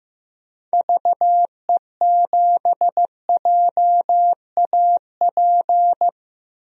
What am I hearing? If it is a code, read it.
Morse code, heard as a keyed tone, about 15 wpm, 700 Hz. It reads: VE7JAP
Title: DXCC-100 Call Signs - 6 Characters - 15wpm